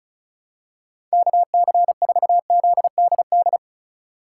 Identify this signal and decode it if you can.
Morse code, heard as a keyed tone, about 36 wpm, 700 Hz. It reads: KC4ZDD
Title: USA Call Signs - 3x Character Spacing - 35wpm